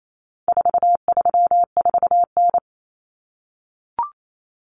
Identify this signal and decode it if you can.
Morse code, heard as a keyed tone, about 28 wpm, 700 Hz. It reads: 434N
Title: Common Cut Numbers - 4-Digit 28wpm